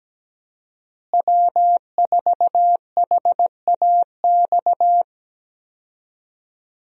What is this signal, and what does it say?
Morse code, heard as a keyed tone, about 17 wpm, 700 Hz. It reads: W4HAX
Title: USA Call Signs - 5 Characters - Repeated 3x - 17wpm